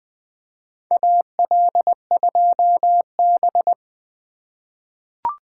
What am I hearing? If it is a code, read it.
Morse code, heard as a keyed tone, about 20 wpm, 700 Hz. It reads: AL2B
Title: USA Call Signs - 4 Characters - 4x Character Spacing - 20wpm